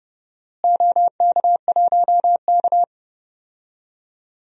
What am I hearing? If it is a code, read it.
Morse code, heard as a keyed tone, about 31 wpm, 700 Hz. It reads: OK1K
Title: DXCC-100 Call Signs - 4 Characters - Repeated 3x - 30wpm